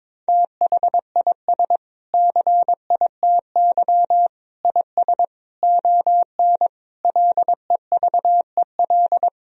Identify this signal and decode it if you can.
Morse code, heard as a keyed tone, about 21 wpm, 700 Hz. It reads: THIS CITY IS ON LEVEL
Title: Sentences from Top 1000 Words - Review Part 4 - 22wpm